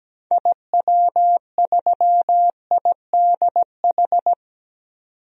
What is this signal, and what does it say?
Morse code, heard as a keyed tone, about 17 wpm, 700 Hz. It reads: IW3IDH
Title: DXCC-100 Call Signs - 6 Characters - 17wpm